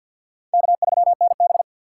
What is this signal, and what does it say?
Morse code, heard as a keyed tone, about 52 wpm, 700 Hz. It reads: K3NB
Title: USA Call Signs - 4 Characters - 3x Character Spacing - 50wpm